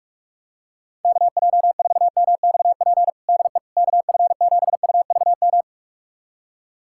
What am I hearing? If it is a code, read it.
Morse code, heard as a keyed tone, about 41 wpm, 700 Hz. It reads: KJ4MXP DE KF7UVM
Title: USA Callsign DE Callsign 45wpm